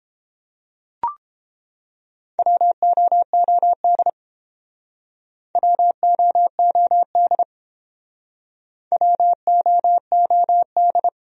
Morse code, heard as a keyed tone, about 30 wpm, 700 Hz. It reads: WOOD WOOD WOOD